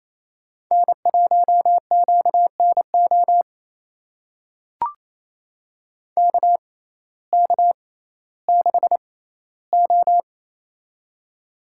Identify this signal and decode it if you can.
Morse code, heard as a keyed tone, about 28 wpm, spaced as for 15 wpm, 700 Hz. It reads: N1QNO KK6O
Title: USA Call Signs - 6x Character Spacing - 28wpm